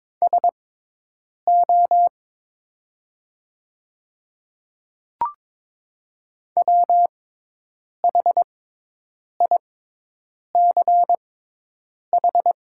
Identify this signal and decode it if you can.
Morse code, heard as a keyed tone, about 22 wpm, spaced as for 8 wpm, 700 Hz. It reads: SO WHICH
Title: 6x Character Spacing - Top 100 Words - 22wpm